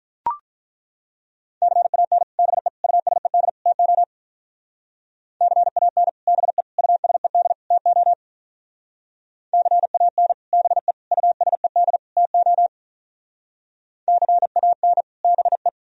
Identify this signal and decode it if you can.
Morse code, heard as a keyed tone, about 42 wpm, 700 Hz. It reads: CAN BE USED TO CAN BE USED TO CAN BE USED TO CAN BE
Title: Speed-Racing - Sets of 4 Words 35wpm